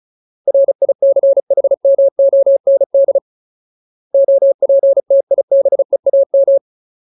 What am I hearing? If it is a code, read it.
Morse code, heard as a keyed tone, about 36 wpm, 550 Hz. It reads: RICHMOND OPTIBEAM
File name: Ultimate Word List / 35WPM Ultimate